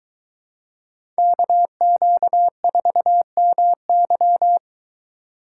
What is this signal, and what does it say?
Morse code, heard as a keyed tone, about 22 wpm, 700 Hz. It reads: KQ4MY